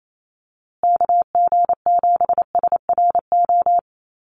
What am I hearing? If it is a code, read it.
Morse code, heard as a keyed tone, about 28 wpm, 700 Hz. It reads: KG7SRO
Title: USA Call Signs Repeated 3x 28wpm